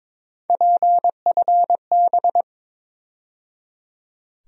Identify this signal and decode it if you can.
Morse code, heard as a keyed tone, about 22 wpm, 700 Hz. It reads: PFB
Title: USA Call Sign Postfixes - 3 Letters 22wpm